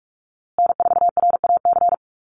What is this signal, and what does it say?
Morse code, heard as a keyed tone, about 44 wpm, 700 Hz. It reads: N4RAC
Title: USA Call Signs Repeated 2x 45wpm